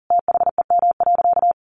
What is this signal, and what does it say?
Morse code, heard as a keyed tone, about 40 wpm, 700 Hz. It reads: THEM.
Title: Sentences from Top 100 Words 40wpm